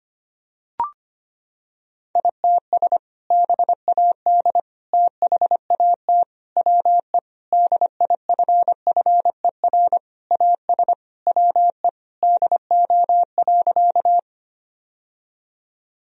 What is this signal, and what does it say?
Morse code, heard as a keyed tone, about 25 wpm, 700 Hz. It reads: ITS BAD THAT WE DIFFER AS WE DO.
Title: Sentences from Top 1000 Words - Review Part 1 - 25wpm